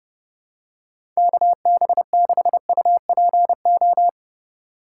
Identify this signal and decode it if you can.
Morse code, heard as a keyed tone, about 30 wpm, 700 Hz. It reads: KB6UPO
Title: USA Call Signs - 6 Characters - 3x Character Spacing - 30wpm